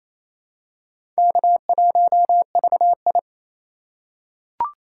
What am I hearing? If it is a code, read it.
Morse code, heard as a keyed tone, about 28 wpm, 700 Hz. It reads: K1VI
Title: USA Call Signs - 4 Characters - 8x Character Spacing - 28wpm